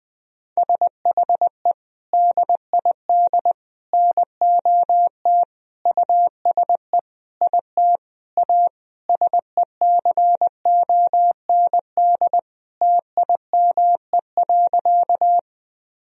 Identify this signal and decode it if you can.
Morse code, heard as a keyed tone, about 21 wpm, 700 Hz. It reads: SHE DID NOT USE IT A SECOND TIME.
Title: Sentences from Top 1000 Words - Review Part 1 - 20wpm